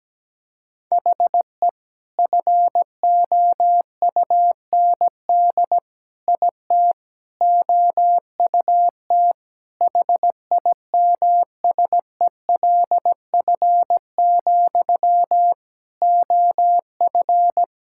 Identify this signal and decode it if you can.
Morse code, heard as a keyed tone, about 17 wpm, 700 Hz. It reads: HE FOUND IT OUT HIMSELF, OF
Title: Sentences from Top 1600 Words 17wpm